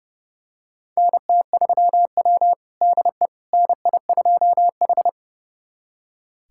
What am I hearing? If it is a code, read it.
Morse code, heard as a keyed tone, about 30 wpm, 700 Hz. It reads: NT3W DE NI2H
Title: USA Callsign DE Callsign - 4 Characters - 30wpm